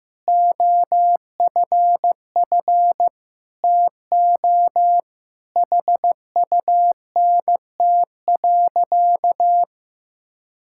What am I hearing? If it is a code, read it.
Morse code, heard as a keyed tone, about 16 wpm, 700 Hz. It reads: OFF TO HUNT.
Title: Sentences from Top 1600 Words 15wpm